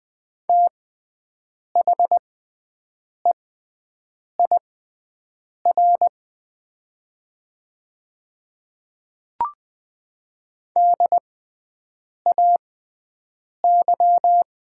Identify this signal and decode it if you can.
Morse code, heard as a keyed tone, about 20 wpm, spaced as for 7 wpm, 690 Hz. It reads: THEIR DAY
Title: 6x Character Spacing - Top 100 Words - 20wpm